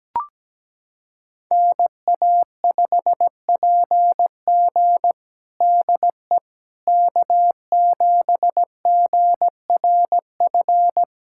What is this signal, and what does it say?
Morse code, heard as a keyed tone, about 17 wpm, 700 Hz. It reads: NA5PG DE K7GRF
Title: USA Callsign DE Callsign - 5 Characters - 17wpm